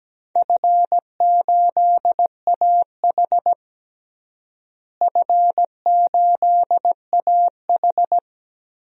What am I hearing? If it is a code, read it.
Morse code, heard as a keyed tone, about 17 wpm, 700 Hz. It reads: F8AH F8AH
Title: DXCC-100 Call Signs - 4 Characters - Repeated 3x - 17wpm